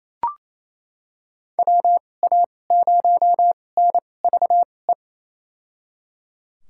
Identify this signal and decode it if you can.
Morse code, heard as a keyed tone, about 28 wpm, spaced as for 20 wpm, 700 Hz. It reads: WA0NVE
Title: USA Call Signs - 6 Characters - 2x Character Spacing - 28wpm